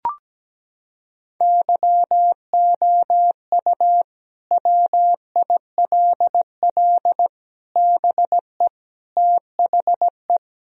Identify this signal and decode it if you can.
Morse code, heard as a keyed tone, about 17 wpm, 700 Hz. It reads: YOU WILL BE THE
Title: Sentences from Top 1000 Words - Review Part 4 - 17wpm